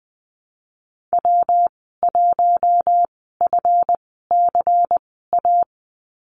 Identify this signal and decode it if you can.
Morse code, heard as a keyed tone, about 20 wpm, spaced as for 15 wpm, 700 Hz. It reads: W1FCA
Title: 173 Lesson 28g - Number 9 - Call Signs 20wpm